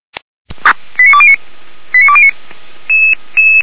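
Five-tone selective call (ZVEI), heard as quick runs of five tones, tones of about 70 ms. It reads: 89209 89209
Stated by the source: ZVEI five-tone selective call 89209 received via SDR.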